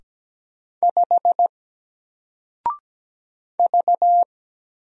Morse code, heard as a keyed tone, about 17 wpm, 700 Hz. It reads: 5 V
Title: Single Letter-Number 17wpm